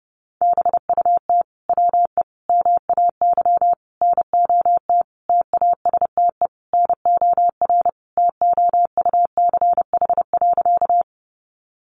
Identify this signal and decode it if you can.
Morse code, heard as a keyed tone, about 28 wpm, 700 Hz. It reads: BUT WE MAY NOT TASTE NOR TOUCH.
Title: Sentences from Top 1500 Words 30wpm